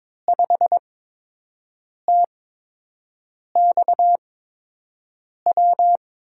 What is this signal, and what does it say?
Morse code, heard as a keyed tone, about 22 wpm, spaced as for 6 wpm, 700 Hz. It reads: 5TXW
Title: USA Call Signs - 5 Characters - 8x Character Spacing - 22wpm